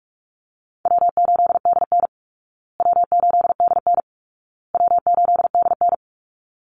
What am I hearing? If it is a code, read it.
Morse code, heard as a keyed tone, about 44 wpm, 700 Hz. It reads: W8DN W8DN W8DN
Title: Call Signs Repeated 2x - 45wpm